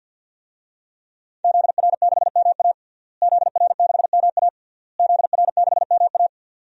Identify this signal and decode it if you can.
Morse code, heard as a keyed tone, about 52 wpm, 700 Hz. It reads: ZR6MA ZR6MA ZR6MA
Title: DXCC-100 Call Signs - 5 Characters - Repeated 3x - 50wpm